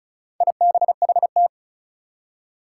Morse code, heard as a keyed tone, about 36 wpm, 700 Hz. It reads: IBHT
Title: Four Challenging Letter-Number Combinations 35wpm